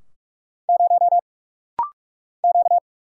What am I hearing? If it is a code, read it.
Morse code, heard as a keyed tone, about 46 wpm, 700 Hz. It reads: 0 Q
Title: Single Letter-Number - Rapid-Fire 45wpm